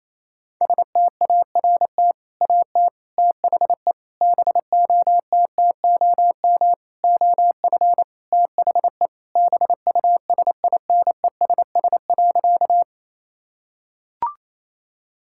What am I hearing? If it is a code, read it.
Morse code, heard as a keyed tone, about 28 wpm, 700 Hz. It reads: START AT THE BOTTOM OF THE BUSINESS.
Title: Sentences from Top 1000 Words - Review Part 6 - 28wpm